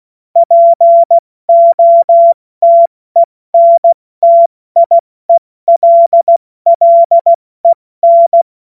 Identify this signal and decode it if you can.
Morse code, heard as a keyed tone, about 16 wpm, 680 Hz. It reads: POTENTIELLEN
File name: Morsekurs / Witze / Witz 10. Mai 2022 70 F80